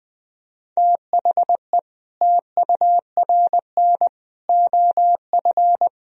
Morse code, heard as a keyed tone, about 20 wpm, 700 Hz. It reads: THE TURN OF
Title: Speed-Racing - Sets of 5 Words 17wpm